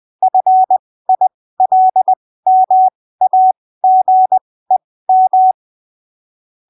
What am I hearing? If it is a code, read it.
Morse code, heard as a keyed tone, about 20 wpm, spaced as for 15 wpm, 750 Hz. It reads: FILMAGEM
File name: texto 09 20ppm